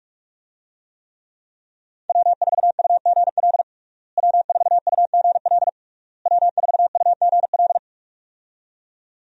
Morse code, heard as a keyed tone, about 47 wpm, 700 Hz. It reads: W4UGL W4UGL W4UGL
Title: USA Call Signs - 5 Characters - Repeated 3x - 45wpm